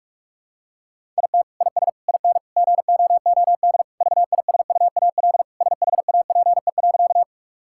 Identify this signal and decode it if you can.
Morse code, heard as a keyed tone, about 46 wpm, 700 Hz. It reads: IT IS IN GOOD VISUAL SHAPE.
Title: Sentences from Top 1000 Words - Review Part 6 - 45wpm